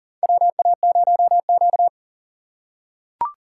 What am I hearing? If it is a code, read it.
Morse code, heard as a keyed tone, about 40 wpm, 700 Hz. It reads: WA0Q
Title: USA Call Signs - 3x Character Spacing - 40wpm